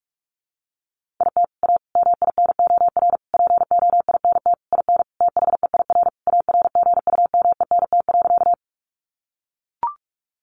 Morse code, heard as a keyed tone, about 44 wpm, 700 Hz. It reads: IT A MINOR POINT IN THEIR ARGUMENT.
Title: Sentences from Top 1500 Words 45wpm